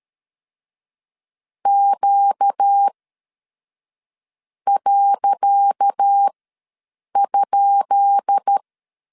Morse code, heard as a keyed tone, about 13 wpm, spaced as for 5 wpm, 800 Hz. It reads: Q.?
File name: LEZIONI / 10 SIGNS / 10C DOT ws 12wpm 3548